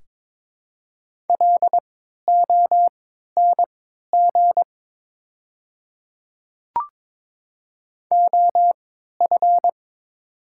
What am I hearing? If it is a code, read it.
Morse code, heard as a keyed tone, about 22 wpm, spaced as for 12 wpm, 700 Hz. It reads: LONG OF
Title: 3x Character Spacing - Top 100 Words - 22wpm